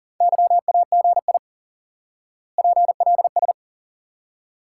Morse code, heard as a keyed tone, about 40 wpm, 700 Hz. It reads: YAGI PLS
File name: IQE Course 3 - Lesson 89 - YAGI - Two Elements - 40wpm